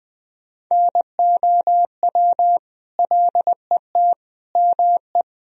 Morse code, heard as a keyed tone, about 20 wpm, 700 Hz. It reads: NOW LET ME
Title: Sentences from Top 300 Words and Abbreviations 20wpm